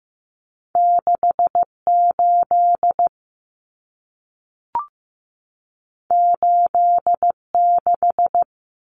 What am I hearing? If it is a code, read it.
Morse code, heard as a keyed tone, about 15 wpm, 700 Hz. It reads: 68 86
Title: Common Cut Numbers - 2-Digit 15wpm